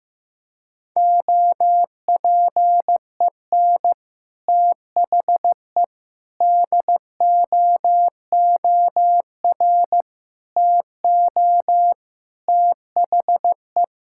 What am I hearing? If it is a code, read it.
Morse code, heard as a keyed tone, about 15 wpm, 700 Hz. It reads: OPEN THE DOOR TO THE
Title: Sentences from Top 1000 Words - Review Part 6 - 15wpm